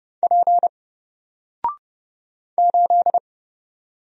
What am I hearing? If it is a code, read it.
Morse code, heard as a keyed tone, about 30 wpm, 700 Hz. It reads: P 8
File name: Single-Letter-Number-30wpm